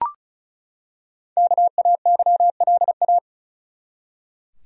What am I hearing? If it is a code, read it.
Morse code, heard as a keyed tone, about 36 wpm, 700 Hz. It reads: KAYLA
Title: USA Common Names - Max 5 Letters - 35wpm